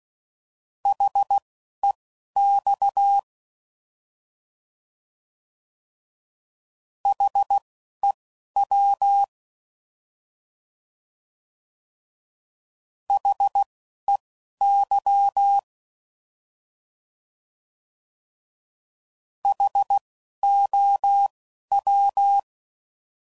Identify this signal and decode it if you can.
Morse code, heard as a keyed tone, about 16 wpm, spaced as for 11 wpm, 800 Hz. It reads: HEX HEW HEY HOW
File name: Prefixes / By First Letter / Hs